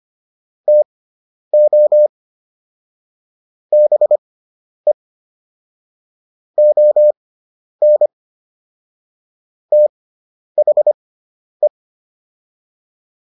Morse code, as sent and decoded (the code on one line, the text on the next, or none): - --- / -... / --- -. / - .... .
TO BE ON THE